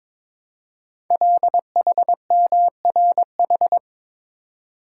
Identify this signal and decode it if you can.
Morse code, heard as a keyed tone, about 22 wpm, 700 Hz. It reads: LHMRH
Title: Code Groups - 5 Letters - 22wpm